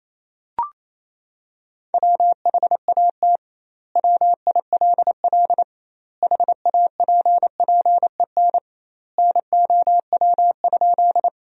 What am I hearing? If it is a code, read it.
Morse code, heard as a keyed tone, about 28 wpm, 700 Hz. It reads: WHAT WILL HAPPEN NOW?
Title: 2x Word-Spacing - Sentences from Top 400 Words - 28wpm